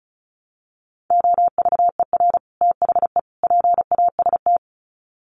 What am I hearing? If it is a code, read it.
Morse code, heard as a keyed tone, about 34 wpm, 700 Hz. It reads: OVER THE PAST
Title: Sets of 3 Words 35wpm